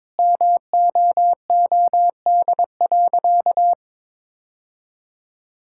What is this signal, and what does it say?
Morse code, heard as a keyed tone, about 22 wpm, 700 Hz. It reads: MOOD.
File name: Difficult Sentences with Hints - 22wpm